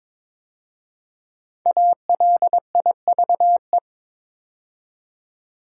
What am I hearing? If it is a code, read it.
Morse code, heard as a keyed tone, about 22 wpm, 700 Hz. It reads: ALIVE
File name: 5 Letter Words - First 3 Letters Given - 22wpm